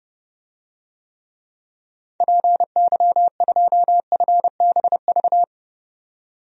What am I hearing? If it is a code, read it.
Morse code, heard as a keyed tone, about 32 wpm, 700 Hz. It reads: PY2FBV
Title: DXCC-100 Call Signs - 6 Characters - 30wpm